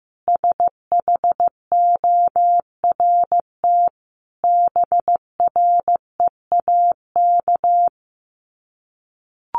Morse code, heard as a keyed tone, about 15 wpm, 700 Hz. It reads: SHORT BREAK
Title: Sets of 5 Words Encore2 15wpm